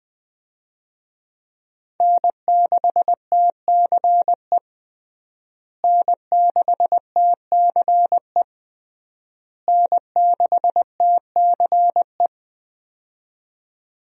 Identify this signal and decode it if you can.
Morse code, heard as a keyed tone, about 20 wpm, 700 Hz. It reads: N6TCE N6TCE N6TCE
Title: USA Call Signs - 5 Characters - Repeated 3x - 20wpm